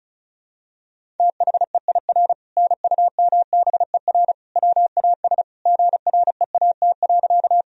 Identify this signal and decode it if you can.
Morse code, heard as a keyed tone, about 35 wpm, 700 Hz. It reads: THEIR NUMBER WAS GREAT.
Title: Sentences from Top 1000 Words - Review Part 1 - 35wpm